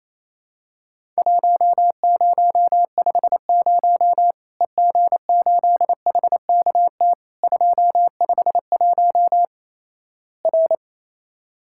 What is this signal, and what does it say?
Morse code, heard as a keyed tone, about 28 wpm, 700 Hz. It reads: 1050 EG8HKT 251 R
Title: Worked All Europe DX Contest - QTC Exchange - 28wpm